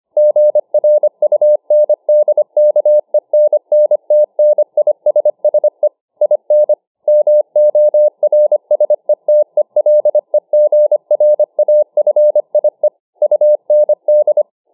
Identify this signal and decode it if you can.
Morse code, heard as a keyed tone, about 25 wpm, 600 Hz. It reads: GRUNDKENNTNISSE IN MORSETELEGRAFIE UND